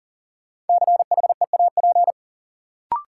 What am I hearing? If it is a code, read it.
Morse code, heard as a keyed tone, about 40 wpm, 700 Hz. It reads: CHEAP
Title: Sentences from Top 1200 Words - New Word Drill - 40wpm